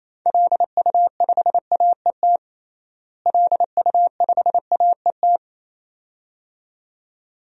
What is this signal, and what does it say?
Morse code, heard as a keyed tone, about 28 wpm, 700 Hz. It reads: LU5AET LU5AET
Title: DXCC-100 Call Signs - 6 Characters - Repeated 2x - 28wpm